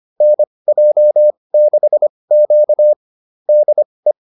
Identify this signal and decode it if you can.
Morse code, heard as a keyed tone, about 25 wpm, spaced as for 20 wpm, 600 Hz. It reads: NJ6Q DE
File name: QSO205 20